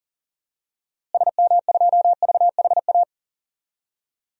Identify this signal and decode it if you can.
Morse code, heard as a keyed tone, about 40 wpm, 700 Hz. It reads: SM2VHA
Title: DXCC-100 Call Signs - 6 Characters - Repeated 2x - 40wpm